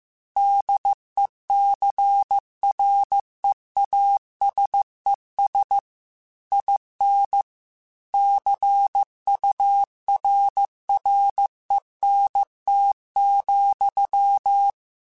Morse code, heard as a keyed tone, about 15 wpm, 800 Hz. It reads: DECREASES IN CURRENT,
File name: Technical Subjects / Technical Subjects-mp3s / XL @13